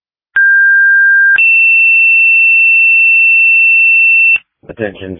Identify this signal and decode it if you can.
Two-tone page, heard as a one-second tone, then a three-second tone, about 1600 then 2700 Hz, with noise. It is heard from 0.4 to 4.4 s.